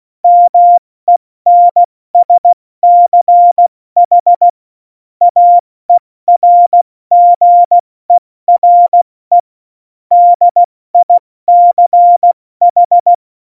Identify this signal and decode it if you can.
Morse code, heard as a keyed tone, about 16 wpm, 700 Hz. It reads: MENSCH AERGERE DICH
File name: Morsekurs / Witze / Witz 20. Oktober 2020, 70 F80